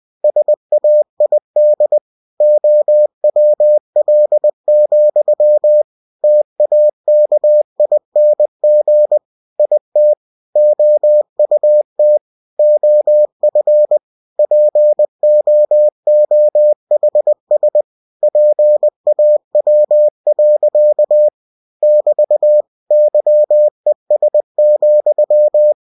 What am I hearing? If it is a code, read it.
Morse code, heard as a keyed tone, about 20 wpm, 600 Hz. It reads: SAID OWL, TAKING IT OUT OF POOHS PAW. <BT> YES,
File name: cw-books / winnie-the-pooh / winnie-20wpm-page-0051